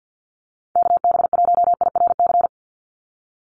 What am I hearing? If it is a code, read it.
Morse code, heard as a keyed tone, about 50 wpm, 700 Hz. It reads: KB1IRC
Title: USA Call Signs 50wpm